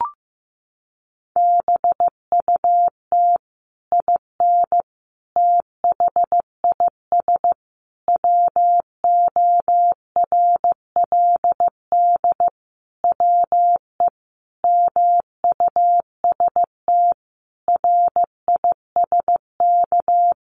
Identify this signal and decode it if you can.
Morse code, heard as a keyed tone, about 15 wpm, 700 Hz. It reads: BUT IN THIS WORLD WE MUST RISK
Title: Sentences from Top 600 Words 15wpm